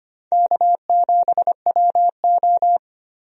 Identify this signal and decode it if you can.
Morse code, heard as a keyed tone, about 25 wpm, 700 Hz. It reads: K7WO
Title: USA Call Signs - 4 Characters - 25wpm